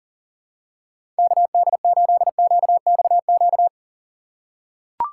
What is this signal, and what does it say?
Morse code, heard as a keyed tone, about 40 wpm, 700 Hz. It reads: KD8QXQ
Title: USA Call Signs - 4x Character Spacing - 40wpm